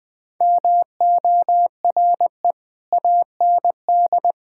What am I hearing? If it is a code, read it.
Morse code, heard as a keyed tone, about 20 wpm, 700 Hz. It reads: MORE AND
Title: Sentences with 3-Word Phrases - Spoken Beginning - 20wpm